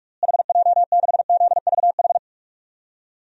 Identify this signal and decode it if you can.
Morse code, heard as a keyed tone, about 46 wpm, 700 Hz. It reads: HJ6ZVH
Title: DXCC-100 Call Signs - 6 Characters - Repeated 3x - 45wpm